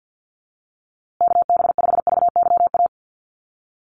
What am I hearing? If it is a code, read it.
Morse code, heard as a keyed tone, about 50 wpm, 700 Hz. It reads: KB5VYA
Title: USA Call Signs Repeated 2x 50wpm